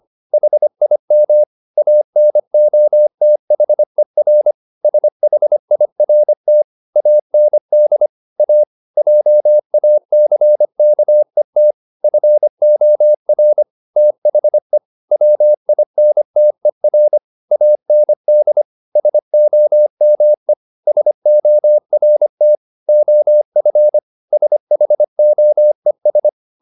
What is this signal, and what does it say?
Morse code, heard as a keyed tone, about 25 wpm, 600 Hz. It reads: HIM ANOTHER SHIRT AND A JACKET FOR THE WINTER AND SOME SORT OF SHOES
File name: the old man and the sea part 1 25wpm